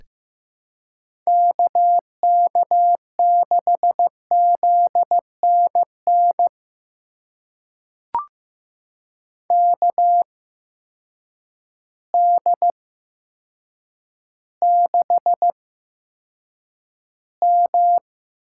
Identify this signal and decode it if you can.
Morse code, heard as a keyed tone, about 15 wpm, spaced as for 8 wpm, 700 Hz. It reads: KK6ZNN KD6M
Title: USA Call Signs - 6 Characters - 8x Character Spacing - 15wpm